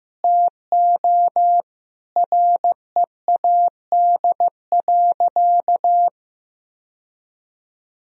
Morse code, heard as a keyed tone, about 15 wpm, 700 Hz. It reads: TO READ.
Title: Sentences from Top 1000 Words - Review Part 1 - 15wpm